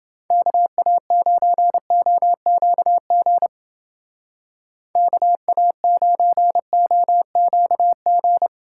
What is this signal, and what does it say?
Morse code, heard as a keyed tone, about 28 wpm, 700 Hz. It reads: KA9OQG KA9OQG